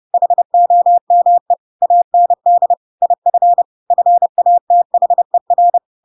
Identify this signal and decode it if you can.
Morse code, heard as a keyed tone, about 30 wpm, 700 Hz. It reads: HOME AND IF FATHER
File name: Pinocchio / Pinocchio ch5 30 30 ik1hge